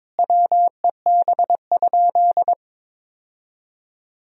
Morse code, heard as a keyed tone, about 22 wpm, 700 Hz. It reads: WEB?
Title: Sentences from Top 1000 Words - Review Part 4 - 22wpm